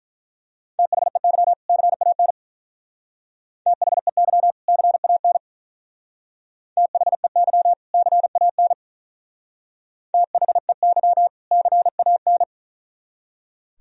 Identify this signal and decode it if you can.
Morse code, heard as a keyed tone, about 44 wpm, 700 Hz. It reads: THEY CAN THEY CAN THEY CAN THEY CAN
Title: Speed-Racing - Sets of 2 Words 35wpm